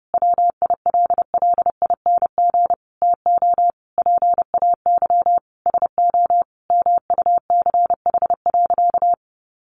Morse code, heard as a keyed tone, about 30 wpm, 700 Hz. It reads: WILLING TO PAY SO MUCH.
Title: Sentences from Top 1400 Words 30wpm